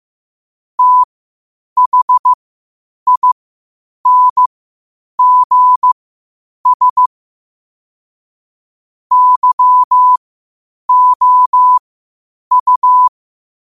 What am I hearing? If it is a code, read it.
Morse code, heard as a keyed tone, about 15 wpm, spaced as for 8 wpm, 1000 Hz. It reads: THINGS YOU
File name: qotdfiles / 8wpm